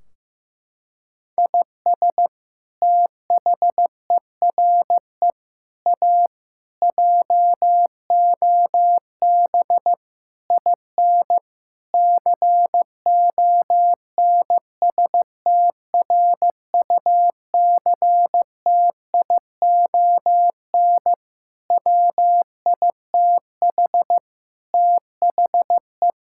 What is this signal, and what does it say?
Morse code, heard as a keyed tone, about 15 wpm, 700 Hz. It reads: IS THERE A JOB IN CONSTRUCTION WITH THE
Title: Sentences from Top 1600 Words 15wpm